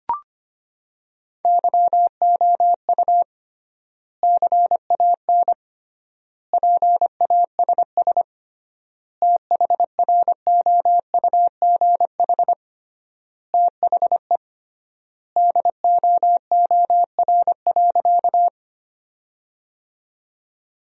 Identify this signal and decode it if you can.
Morse code, heard as a keyed tone, about 25 wpm, 700 Hz. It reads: YOU CAN PASS THROUGH THE DOOR.
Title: 3x Word-Spacing - Sentences from Top 400 Words - 25wpm